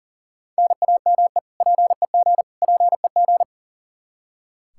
Morse code, heard as a keyed tone, about 37 wpm, 700 Hz. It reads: NAME PEG PEG
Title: Practice QSO - Exchange 1 - Beginning - 40wpm